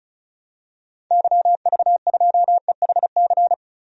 Morse code, heard as a keyed tone, about 36 wpm, 700 Hz. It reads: YV2EHC
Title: DXCC-100 Call Signs - 6 Characters - 35wpm